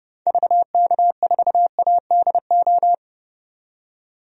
Morse code, heard as a keyed tone, about 30 wpm, 690 Hz. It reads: VK4ADO
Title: DXCC-100 Call Signs - 6 Characters - 30wpm